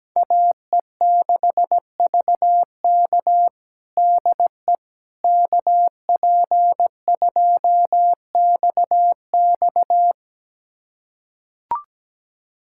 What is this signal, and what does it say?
Morse code, heard as a keyed tone, about 17 wpm, 700 Hz. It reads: AE6VK DE KP2XX
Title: USA Callsign DE Callsign - 5 Characters - 17wpm